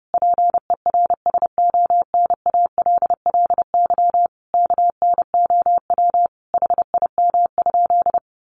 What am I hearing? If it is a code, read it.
Morse code, heard as a keyed tone, about 30 wpm, 700 Hz. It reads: PERSONALLY KNOW HIM?
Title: Sentences from Top 1400 Words 30wpm